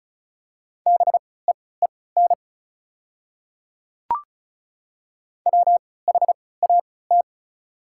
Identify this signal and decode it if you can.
Morse code, heard as a keyed tone, about 36 wpm, spaced as for 20 wpm, 700 Hz. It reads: BEEN WHAT
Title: 3x Character Spacing - Top 200 Words - 35wpm